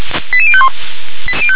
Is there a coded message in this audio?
Yes: five-tone selective call (ZVEI) from 0.3 to 0.7 s, heard as quick runs of five tones, tones of about 70 ms. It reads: 88051